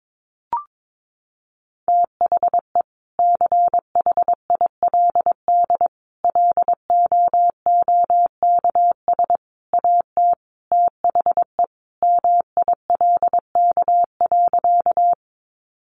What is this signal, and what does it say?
Morse code, heard as a keyed tone, about 22 wpm, 700 Hz. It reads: THE CHILD LOOKS AT THE MILK.
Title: Additional Easy Phrases 22wpm